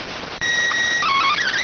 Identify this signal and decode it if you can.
SSTV (slow-scan television), PD 290.